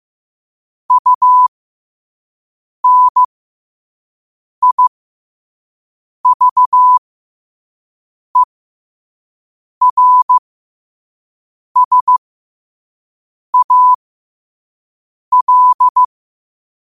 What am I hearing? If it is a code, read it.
Morse code, heard as a keyed tone, about 15 wpm, spaced as for 5 wpm, 1000 Hz. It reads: UNIVERSAL
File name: qotdfiles / 5wpm